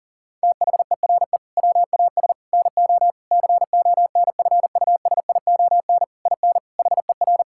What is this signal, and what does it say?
Morse code, heard as a keyed tone, about 40 wpm, 700 Hz. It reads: THERE WAS NO CONFUSION IN HER
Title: Sentences from Top 1600 Words 40wpm